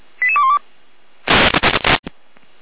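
Five-tone selective call (ZVEI), heard as quick runs of five tones, tones of about 80 ms. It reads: 88212
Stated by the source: ZVEI five-tone call 88137 received by SDR.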